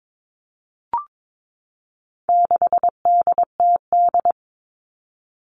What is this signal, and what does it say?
Morse code, heard as a keyed tone, about 22 wpm, 700 Hz. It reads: 6DTD
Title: All Cut Numbers - 4-Digit 22wpm